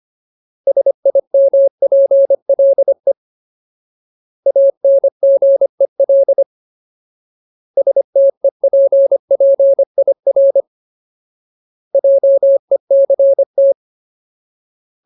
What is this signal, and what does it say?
Morse code, heard as a keyed tone, about 25 wpm, 550 Hz. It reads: SIMPLE ANGEL STEPPIR JECT
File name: Ultimate Word List / 25WPM Ultimate